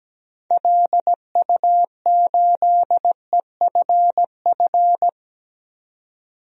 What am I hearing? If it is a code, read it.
Morse code, heard as a keyed tone, about 17 wpm, 700 Hz. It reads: LU8EFF
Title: DXCC-100 Call Signs - 6 Characters - 17wpm